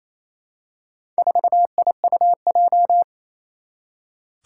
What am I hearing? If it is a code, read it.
Morse code, heard as a keyed tone, about 27 wpm, 700 Hz. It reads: 4IUJ
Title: Four Challenging Letter-Number Combinations 28wpm